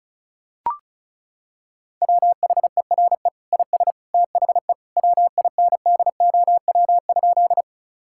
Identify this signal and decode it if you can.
Morse code, heard as a keyed tone, about 34 wpm, 700 Hz. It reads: WHERE IS THE WINDOW?
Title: Sentences from Top 1600 Words 35wpm